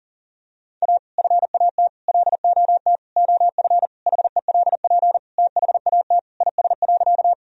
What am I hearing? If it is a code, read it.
Morse code, heard as a keyed tone, about 40 wpm, 700 Hz. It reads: A FAT LOT OF HELP THAT IS.
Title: Sentences from Top 1000 Words - Review Part 4 - 40wpm